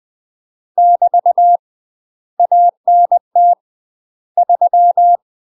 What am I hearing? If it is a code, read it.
Morse code, heard as a keyed tone, about 20 wpm, 700 Hz. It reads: <BT> ANT 3
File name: UK MASTER QSOs TRAINING/20WPM